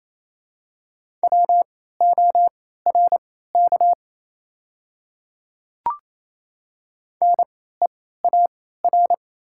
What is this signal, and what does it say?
Morse code, heard as a keyed tone, about 28 wpm, spaced as for 16 wpm, 700 Hz. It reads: WORK NEAR
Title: 3x Character Spacing - Top 200 Words - 28wpm